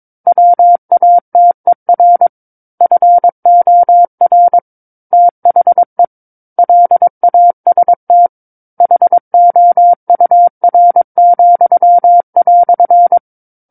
Morse code, heard as a keyed tone, about 22 wpm, 700 Hz. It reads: WATER FOR THE LAST HOUR,"
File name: Book One / 13-Chapter 13 - 22 WPM